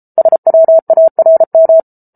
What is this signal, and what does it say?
Morse code, heard as a keyed tone, about 34 wpm, 650 Hz. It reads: SWARM